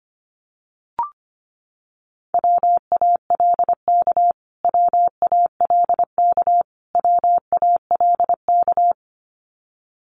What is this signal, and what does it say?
Morse code, heard as a keyed tone, about 25 wpm, 700 Hz. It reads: WALK WALK WALK
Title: Top 300 Words Repeated 25wpm